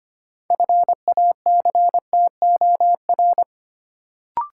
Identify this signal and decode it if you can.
Morse code, heard as a keyed tone, about 25 wpm, 700 Hz. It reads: FACTOR
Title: Sentences from Top 1300 Words - New Word Drill Alt - 25wpm